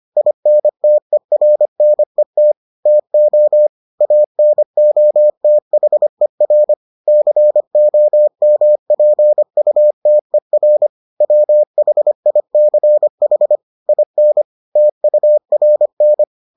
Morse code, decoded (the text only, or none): INTERNET TO ANOTHER COMPUTER WHICH IN TURN